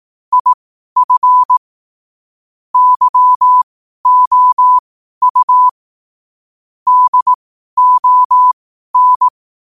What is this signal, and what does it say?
Morse code, heard as a keyed tone, about 18 wpm, spaced as for 12 wpm, 1000 Hz. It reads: IF YOU DON
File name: qotdfiles / 12wpm